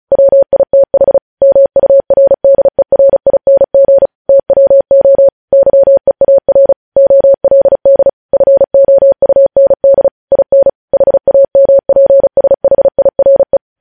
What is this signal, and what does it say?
Morse code, heard as a keyed tone, about 36 wpm, 550 Hz. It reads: WITH MURDERING TWO YEAR OLD FOUND IN HAMPSHIRE
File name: podcast / News headlines-1x-35x35